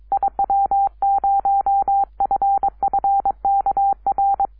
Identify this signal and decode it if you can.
Morse code, heard as a keyed tone, about 23 wpm, 790 Hz. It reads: IW0FFK R
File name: iw0ffk / 2006 06 10 w8gg iw0ffk